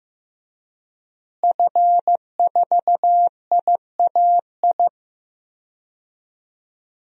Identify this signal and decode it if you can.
Morse code, heard as a keyed tone, about 15 wpm, 700 Hz. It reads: F4IAI
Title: DXCC-100 Call Signs - 5 Characters - Repeated 3x - 15wpm